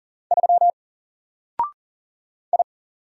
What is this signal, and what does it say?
Morse code, heard as a keyed tone, about 39 wpm, 700 Hz. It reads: 3 I
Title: Single Character (Letter:Number:Punctuation:Prosign) 40wpm